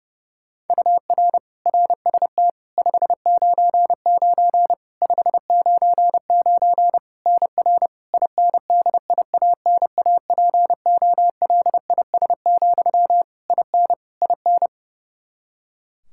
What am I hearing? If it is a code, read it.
Morse code, heard as a keyed tone, about 30 wpm, 700 Hz. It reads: UR RST 599 599 NR INDIANAPOLIS, IN IN
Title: Practice QSO - Exchange 1 - Beginning - 30wpm